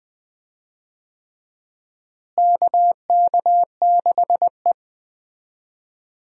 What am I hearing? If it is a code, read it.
Morse code, heard as a keyed tone, about 20 wpm, 700 Hz. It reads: KK6E
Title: USA Call Signs - 4 Characters - Repeated 3x - 20wpm